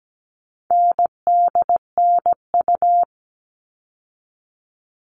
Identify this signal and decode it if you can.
Morse code, heard as a keyed tone, about 17 wpm, 700 Hz. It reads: NDNU
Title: All Cut Numbers - 4-Digit 17wpm